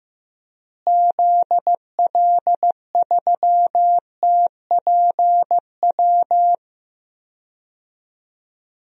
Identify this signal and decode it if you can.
Morse code, heard as a keyed tone, about 15 wpm, 700 Hz. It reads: ZL3TPW
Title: DXCC-100 Call Signs - 6 Characters - Repeated 3x - 15wpm